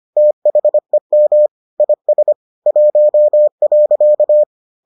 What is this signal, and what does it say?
Morse code, heard as a keyed tone, about 25 wpm, 600 Hz. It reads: THEM IS 1.